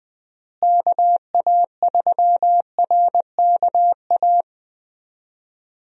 Morse code, heard as a keyed tone, about 20 wpm, 700 Hz. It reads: KA3RKA
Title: USA Call Signs - 6 Characters - Repeated 2x - 20wpm